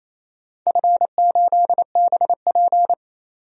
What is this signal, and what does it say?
Morse code, heard as a keyed tone, about 27 wpm, 700 Hz. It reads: F8BP
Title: DXCC-100 Call Signs - 4 Characters - 28wpm